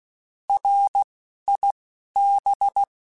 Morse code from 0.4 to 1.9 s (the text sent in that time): R I